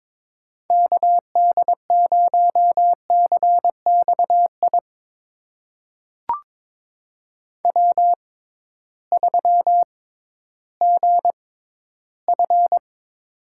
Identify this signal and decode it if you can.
Morse code, heard as a keyed tone, about 22 wpm, spaced as for 13 wpm, 700 Hz. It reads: KD0CXI W3GF
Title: USA Call Signs - 6x Character Spacing - 22wpm